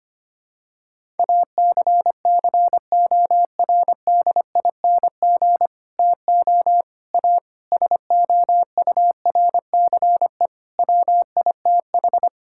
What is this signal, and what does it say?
Morse code, heard as a keyed tone, about 25 wpm, 700 Hz. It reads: ACCORDING TO A SOURCE WITH
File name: practice / 25wpm / Sentences with 9-Word Phrases - 25wpm